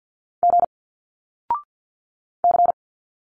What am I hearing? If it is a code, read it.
Morse code, heard as a keyed tone, about 51 wpm, 700 Hz. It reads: G C
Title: Single Letters 50wpm